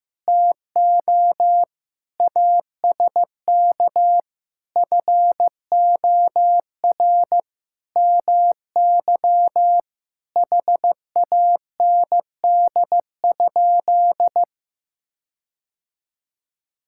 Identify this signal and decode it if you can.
Morse code, heard as a keyed tone, about 16 wpm, 700 Hz. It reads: TO ASK FOR MY HAND?
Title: Sentences from Top 1600 Words 15wpm